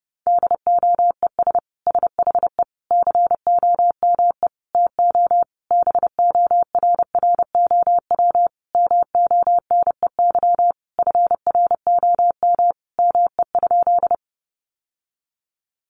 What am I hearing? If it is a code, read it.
Morse code, heard as a keyed tone, about 28 wpm, 700 Hz. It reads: DOES SHE COME TO BORROW MONEY FROM ME?
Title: Sentences from Top 1500 Words 30wpm